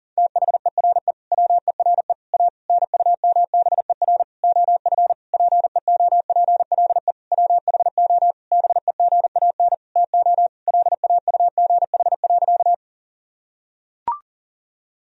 Morse code, heard as a keyed tone, about 37 wpm, 700 Hz. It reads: THERE WERE A NUMBER OF PEOPLE WHO BEGAN TO LAUGH.
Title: Sentences from Top 1000 Words - Review Part 1 - 40wpm